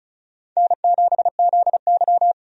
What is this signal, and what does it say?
Morse code, heard as a keyed tone, about 36 wpm, 700 Hz. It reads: N7ZY
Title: USA Call Signs - 4 Characters - 35wpm